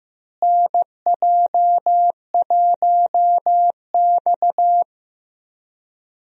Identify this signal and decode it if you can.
Morse code, heard as a keyed tone, about 16 wpm, 700 Hz. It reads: NJ1X
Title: USA Call Signs - 4 Characters - Repeated 3x - 15wpm